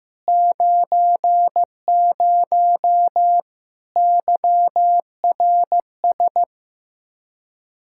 Morse code, heard as a keyed tone, about 15 wpm, 700 Hz. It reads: 90 YRS
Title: K1USN SST QSO Practice 15wpm